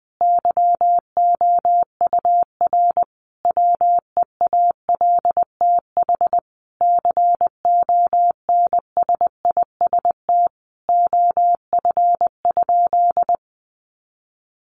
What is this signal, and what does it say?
Morse code, heard as a keyed tone, about 20 wpm, 700 Hz. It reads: YOUR WEALTH CONSIST OF?
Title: Sentences from Top 1400 Words 20wpm